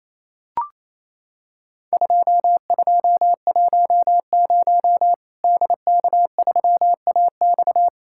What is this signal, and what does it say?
Morse code, heard as a keyed tone, about 28 wpm, 700 Hz. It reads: 2210 DK3AX
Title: Worked All Europe DX Contest - QTC Exchange - 28wpm